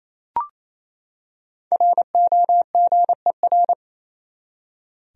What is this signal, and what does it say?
Morse code, heard as a keyed tone, about 28 wpm, 700 Hz. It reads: ROGER
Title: USA Common Names - Max 5 Letters - 28wpm